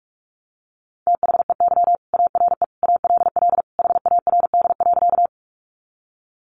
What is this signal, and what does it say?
Morse code, heard as a keyed tone, about 44 wpm, 700 Hz. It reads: THEY ARE ALL HARD.
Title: Additional Easy Phrases 45wpm